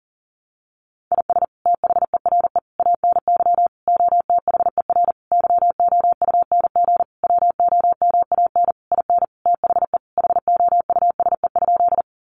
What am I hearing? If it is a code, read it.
Morse code, heard as a keyed tone, about 40 wpm, 700 Hz. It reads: IS THERE ANY OTHER YOUNG WOMAN IN THE HOUSE?
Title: Sentences from Top 1100 Words 40wpm